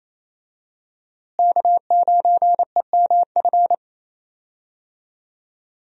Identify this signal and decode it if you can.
Morse code, heard as a keyed tone, about 28 wpm, 700 Hz. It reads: K9EMF
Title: USA Call Signs - 5 Characters - Repeated 2x - 28wpm